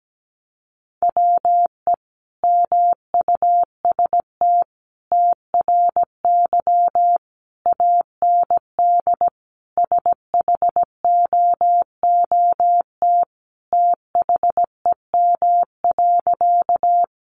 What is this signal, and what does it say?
Morse code, heard as a keyed tone, about 17 wpm, 700 Hz. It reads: WE MUST TRY AND SHOOT THEM.
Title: Sentences from Top 1200 Words 17wpm